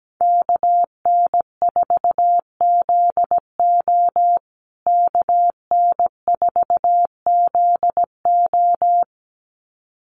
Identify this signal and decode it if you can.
Morse code, heard as a keyed tone, about 17 wpm, 700 Hz. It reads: KN4ZO KN4ZO
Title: USA Call Signs Repeated 2x 17wpm